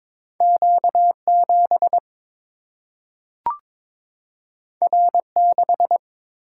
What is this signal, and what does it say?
Morse code, heard as a keyed tone, about 22 wpm, 700 Hz. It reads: Q 7 R 6
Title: Single-Letter Single-Number 22wpm